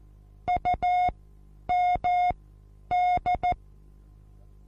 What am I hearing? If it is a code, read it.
Morse code, heard as a keyed tone, about 14 wpm, spaced as for 9 wpm, 720 Hz. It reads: UMD
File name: Onlinelehrgang / morsen / DJ4UF Morsen23